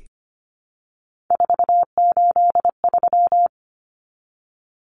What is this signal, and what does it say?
Morse code, heard as a keyed tone, about 25 wpm, 700 Hz. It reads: 483
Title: Numbers - 3-Digit 25wpm